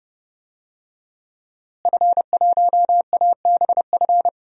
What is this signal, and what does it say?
Morse code, heard as a keyed tone, about 30 wpm, 700 Hz. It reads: F1ABF
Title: DXCC-100 Call Signs - 5 Characters - Repeated 3x - 30wpm